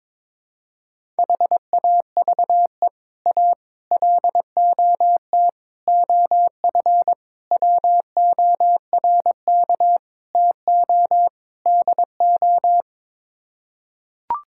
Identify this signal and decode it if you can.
Morse code, heard as a keyed tone, about 22 wpm, 700 Hz. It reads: HAVE A LOT OF WORK TO DO
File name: practice / 22wpm / Sets of 7 Words - 22wpm